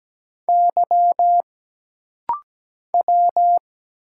Morse code, heard as a keyed tone, about 17 wpm, 700 Hz. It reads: Y W
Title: Single Letters - Rapid-Fire 17wpm